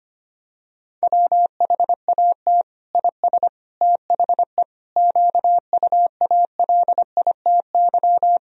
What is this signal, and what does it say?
Morse code, heard as a keyed tone, about 25 wpm, 700 Hz. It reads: WHAT IS THE QUALITY
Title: Sentences from Top 1000 Words - Review Part 4 - 25wpm